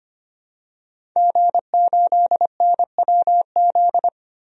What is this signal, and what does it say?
Morse code, heard as a keyed tone, about 25 wpm, 700 Hz. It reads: G8NWZ
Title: DXCC-100 Call Signs - 5 Characters - 25wpm